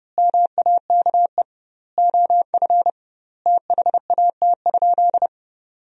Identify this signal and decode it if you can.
Morse code, heard as a keyed tone, about 30 wpm, 700 Hz. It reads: MAKE OF THAT?
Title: 2x Word-Spacing - Sentences from Top 100 Words - 30wpm